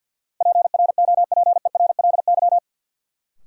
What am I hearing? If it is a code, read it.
Morse code, heard as a keyed tone, about 46 wpm, 700 Hz. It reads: PROPERLY
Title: Sentences from Top 1100 Words - New Word Drill Alt - 50wpm